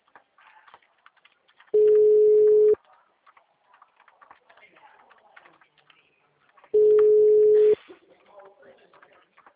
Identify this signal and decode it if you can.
Ringback tone (Europe, 425 Hz), with speech.